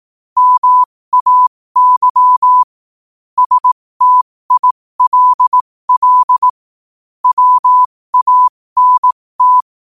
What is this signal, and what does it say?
Morse code, heard as a keyed tone, about 18 wpm, spaced as for 15 wpm, 1000 Hz. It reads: MAY STILL WANT
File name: qotdfiles / 15wpm